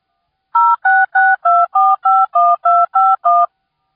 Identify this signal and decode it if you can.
DTMF, heard as telephone touch tones, tones of about 220 ms. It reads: 0662451251